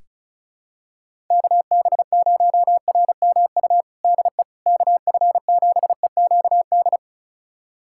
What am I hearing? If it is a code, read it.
Morse code, heard as a keyed tone, about 36 wpm, 700 Hz. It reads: KB0RMU DE KF7EQD
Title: USA Callsign DE Callsign - 6 Characters - 35wpm